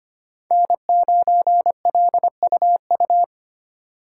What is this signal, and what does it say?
Morse code, heard as a keyed tone, about 25 wpm, 700 Hz. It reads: N9LUU
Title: USA Call Signs - 5 Characters - 4x Character Spacing - 25wpm